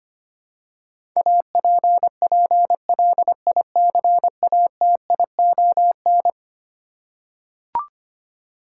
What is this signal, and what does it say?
Morse code, heard as a keyed tone, about 25 wpm, 700 Hz. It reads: APPLICATION
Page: morse-code-ninja-practice-25wpm